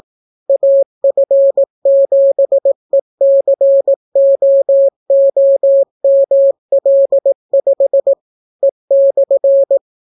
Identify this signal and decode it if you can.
Morse code, heard as a keyed tone, about 18 wpm, 550 Hz. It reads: AF7ECOOML5 E/